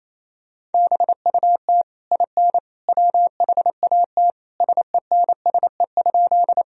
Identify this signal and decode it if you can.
Morse code, heard as a keyed tone, about 28 wpm, 700 Hz. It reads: BUT IN WHAT SENSE?
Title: Sentences from Top 1000 Words - Review Part 4 - 28wpm